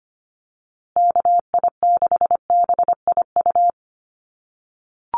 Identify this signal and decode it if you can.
Morse code, heard as a keyed tone, about 25 wpm, 700 Hz. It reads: KI6BIU
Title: USA Call Signs Repeated 3x 25wpm